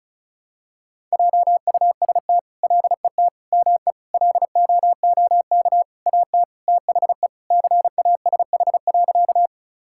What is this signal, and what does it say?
Morse code, heard as a keyed tone, about 36 wpm, 700 Hz. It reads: JUST LET ME LOOK AT THE CASH.
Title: Sentences from Top 1000 Words - Review Part 5 - 35wpm